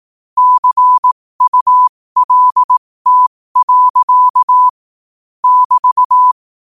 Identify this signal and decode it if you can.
Morse code, heard as a keyed tone, about 18 wpm, spaced as for 15 wpm, 1000 Hz. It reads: CULT. <BT>
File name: qotdfiles / 15wpm